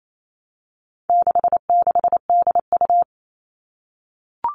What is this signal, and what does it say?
Morse code, heard as a keyed tone, about 28 wpm, 700 Hz. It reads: 66DU
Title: All Cut Numbers - 4-Digit 28wpm